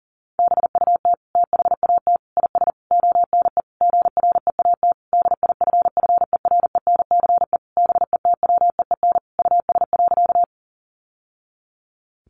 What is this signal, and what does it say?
Morse code, heard as a keyed tone, about 40 wpm, 700 Hz. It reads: BUT THAT IS ONE GREAT DIFFERENCE BETWEEN US.
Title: Sentences from Top 800 Words 40wpm